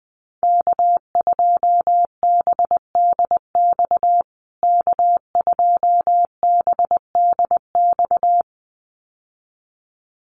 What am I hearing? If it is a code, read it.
Morse code, heard as a keyed tone, about 20 wpm, 700 Hz. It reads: K2BDX K2BDX
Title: USA Call Signs Repeated 2x 20wpm